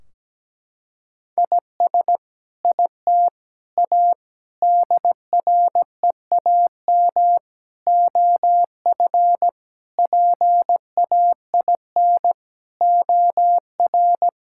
Morse code, heard as a keyed tone, about 17 wpm, 700 Hz. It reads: IS IT A DREAM OF PAIN OR
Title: Sentences from Top 1600 Words 17wpm